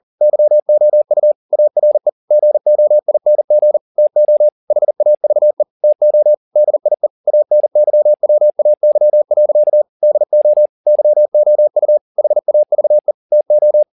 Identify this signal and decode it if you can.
Morse code, heard as a keyed tone, about 40 wpm, 600 Hz. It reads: YOU ARE GOING TO HAVE TO DIE ANYWAY. DO YOU HAVE TO
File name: the old man and the sea 2 40wpm